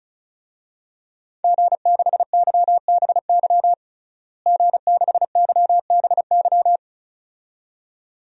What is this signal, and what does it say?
Morse code, heard as a keyed tone, about 36 wpm, 700 Hz. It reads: G6YBY G6YBY
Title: DXCC-100 Call Signs - 5 Characters - Repeated 2x - 35wpm